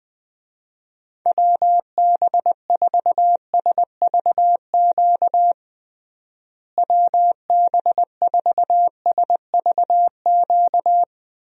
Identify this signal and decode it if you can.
Morse code, heard as a keyed tone, about 20 wpm, 700 Hz. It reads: WB4SVQ WB4SVQ
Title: USA Call Signs - 6 Characters - Repeated 3x - 20wpm